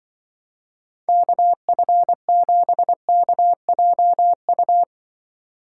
Morse code, heard as a keyed tone, about 23 wpm, 700 Hz. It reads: KF7KJU
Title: Speed Racing - USA Call Signs 20wpm